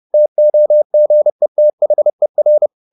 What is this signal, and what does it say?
Morse code, heard as a keyed tone, about 30 wpm, 600 Hz. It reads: TOGETHER
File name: the old man and the sea / the old man and the sea 1 30wpm